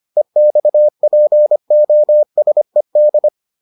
Morse code, heard as a keyed tone, about 25 wpm, 600 Hz. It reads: EXPOSED